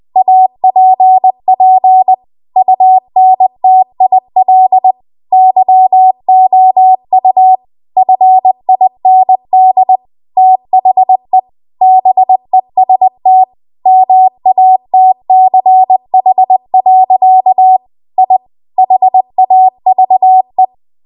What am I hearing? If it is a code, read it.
Morse code, heard as a keyed tone, about 20 wpm, 750 Hz. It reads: APP UNTIL YOU FIND THE BEST MATCH. I HAVE
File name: Archive / 20 WPM / 20WPM